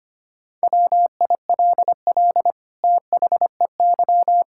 Morse code, heard as a keyed tone, about 25 wpm, 700 Hz. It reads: WILL THEY
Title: Sentences from Top 1000 Words - Review Part 5 - 25wpm